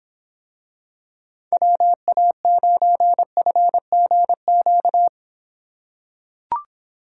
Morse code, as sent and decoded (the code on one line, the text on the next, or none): .-- .- ----. ..-. --. --.-
WA9FGQ